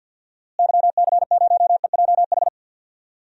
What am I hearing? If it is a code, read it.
Morse code, heard as a keyed tone, about 50 wpm, 700 Hz. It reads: YC0EJH
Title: DXCC-100 Call Signs - 6 Characters - Repeated 3x - 50wpm